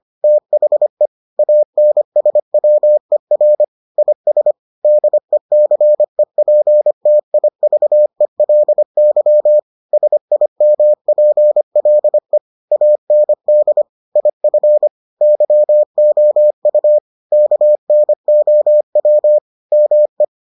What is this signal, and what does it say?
Morse code, heard as a keyed tone, about 25 wpm, 600 Hz. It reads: THE ANSWER IS DECEPTIVELY SIMPLE AND IF YOU KNOW ME